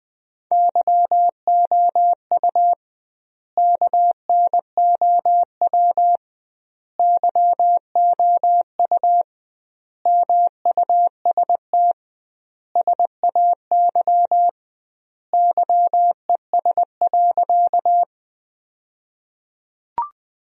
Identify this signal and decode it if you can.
Morse code, heard as a keyed tone, about 20 wpm, 700 Hz. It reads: YOU KNOW YOU MUST SAY YES.
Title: 2x Word-Spacing - Sentences from Top 500 Words - 20wpm